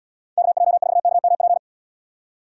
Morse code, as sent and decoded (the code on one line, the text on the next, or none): -.- .--- ....- -.- -- .--.
KJ4KMP